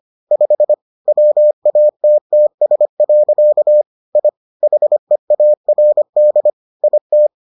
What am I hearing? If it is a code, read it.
Morse code, heard as a keyed tone, about 25 wpm, 600 Hz. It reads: 5 WATTS. I HEARD IT